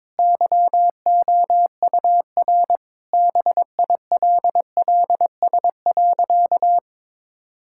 Morse code, heard as a keyed tone, about 22 wpm, 690 Hz. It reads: YOUR BILLS.
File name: practice / 22wpm / Easy Sentences with Hints - 22wpm